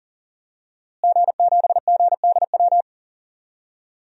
Morse code, heard as a keyed tone, about 40 wpm, 700 Hz. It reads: G7GDW
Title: DXCC-100 Call Signs - 5 Characters - Repeated 2x - 40wpm